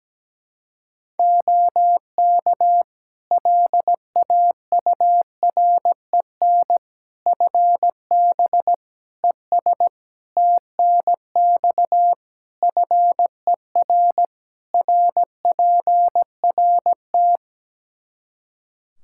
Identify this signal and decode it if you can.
Morse code, heard as a keyed tone, about 17 wpm, 700 Hz. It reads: OK LAUREN FB ES TNX FER RPRT
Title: Practice QSO - Exchange 2 - Info - 17wpm